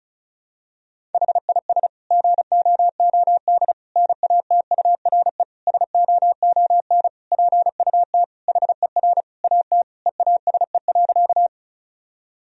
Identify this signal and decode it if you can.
Morse code, heard as a keyed tone, about 34 wpm, 700 Hz. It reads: HIS GOOD NATURE SOON PUT HER AT EASE.
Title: Sentences from Top 1000 Words - Review Part 1 - 35wpm